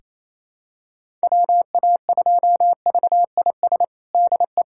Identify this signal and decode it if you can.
Morse code, heard as a keyed tone, about 28 wpm, 700 Hz. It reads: WA2VIS DE
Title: USA Callsign DE Callsign - 6 Characters - 28wpm